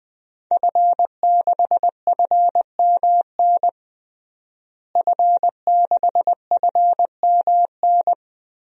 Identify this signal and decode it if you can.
Morse code, heard as a keyed tone, about 20 wpm, 700 Hz. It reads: F6FMN F6FMN
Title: DXCC-100 Call Signs - 5 Characters - Repeated 3x - 20wpm